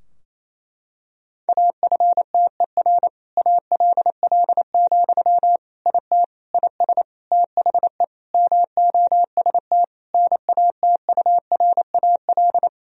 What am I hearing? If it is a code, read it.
Morse code, heard as a keyed tone, about 27 wpm, 700 Hz. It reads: AFTER ALL, IT IS THE MOST NATURAL
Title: Sentences from Top 1000 Words - Review Part 4 - 28wpm